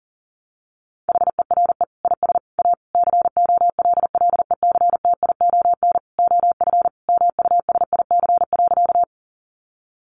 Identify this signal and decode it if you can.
Morse code, heard as a keyed tone, about 40 wpm, 700 Hz. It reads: HERE IS A COLLECTION OF MUSIC.
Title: Sentences from Top 1200 Words 40wpm